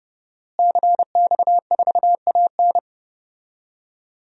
Morse code, heard as a keyed tone, about 31 wpm, 700 Hz. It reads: CX4AN
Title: DXCC-100 Call Signs - 5 Characters - Repeated 2x - 30wpm